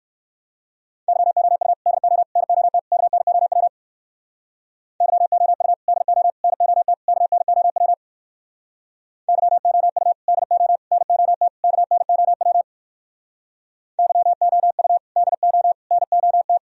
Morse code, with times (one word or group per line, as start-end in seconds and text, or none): 1.1-1.7 YOU
1.9-2.2 DO
2.4-2.8 NOT
2.9-3.7 KNOW
5.0-5.7 YOU
5.9-6.3 DO
6.4-6.9 NOT
7.1-7.9 KNOW
9.3-10.1 YOU
10.3-10.8 DO
10.9-11.5 NOT
11.6-12.6 KNOW
14.0-15.0 YOU
15.2-15.7 DO
15.9-16.6 NOT